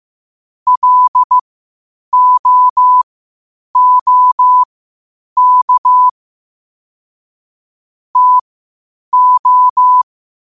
Morse code, heard as a keyed tone, about 15 wpm, spaced as for 8 wpm, 1000 Hz. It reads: LOOK TO
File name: qotdfiles / 8wpm